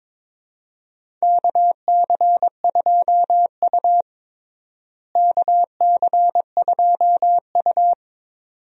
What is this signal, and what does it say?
Morse code, heard as a keyed tone, about 22 wpm, 700 Hz. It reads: KC2U KC2U
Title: USA Call Signs - 4 Characters - Repeated 2x - 22wpm